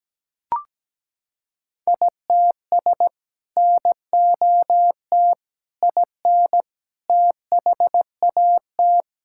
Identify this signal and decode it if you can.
Morse code, heard as a keyed tone, about 17 wpm, 700 Hz. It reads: ITS NOT IN THAT
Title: Sentences from Top 1000 Words - Review Part 5 - 17wpm